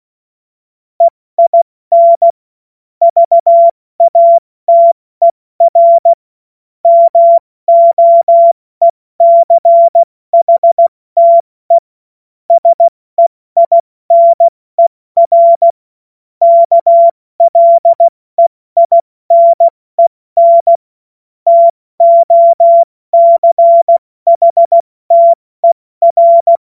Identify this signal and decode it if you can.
Morse code, heard as a keyed tone, about 16 wpm, spaced as for 14 wpm, 680 Hz. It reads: EIN VATER MOECHTE SEINER KLEINEN TOCHTER
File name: Morsekurs / Witze / Witz 14. März 2023 70 F80